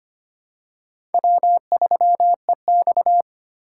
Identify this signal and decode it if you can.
Morse code, heard as a keyed tone, about 25 wpm, 700 Hz. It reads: W3EX
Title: USA Call Signs - 4 Characters - 2x Character Spacing - 25wpm